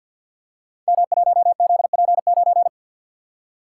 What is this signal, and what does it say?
Morse code, heard as a keyed tone, about 53 wpm, 700 Hz. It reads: M1ZP9
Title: Code Groups - 5 Letter:Numbers - 50wpm